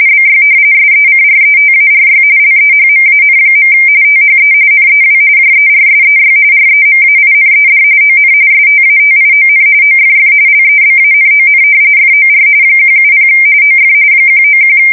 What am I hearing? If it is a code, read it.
RTTY, about 45.45 baud. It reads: BY SECRETARY LEHMAN. THE NEW JERSEY REMAINED ON STATION OFF LEBANON FOR 170 OF 191
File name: fsk170 / battleshipnj